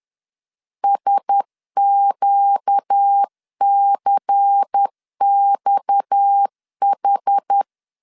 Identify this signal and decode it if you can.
Morse code, heard as a keyed tone, about 10 wpm, 800 Hz. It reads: SQCXH
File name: CW / DET / 1 6 1500C S C 10wpm 4252